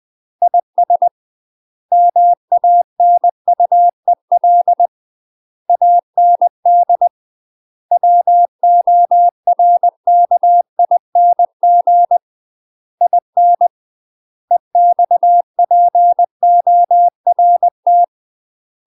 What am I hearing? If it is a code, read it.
Morse code, heard as a keyed tone, about 20 wpm, 700 Hz. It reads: IS MANUEL AND WORKING IN EXPORT